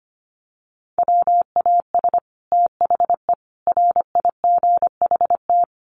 Morse code, heard as a keyed tone, about 25 wpm, 700 Hz. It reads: WAS THE RIGHT
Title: Sets of 5 Words Encore2 25wpm